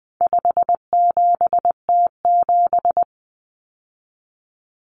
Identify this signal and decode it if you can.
Morse code, heard as a keyed tone, about 20 wpm, 700 Hz. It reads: 57T7
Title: Common Cut Numbers - 4-Digit 20wpm